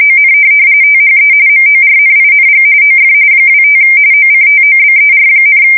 RTTY, about 45.45 baud. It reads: ON THE MORNING OF 25 OCTOBER,